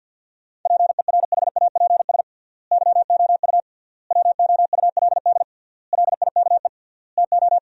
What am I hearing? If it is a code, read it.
Morse code, heard as a keyed tone, about 50 wpm, 700 Hz. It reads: PERHAPS YOU WOULD LIKE TO
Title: 3x Word-Spacing - Sentences from Top 500 Words - 50wpm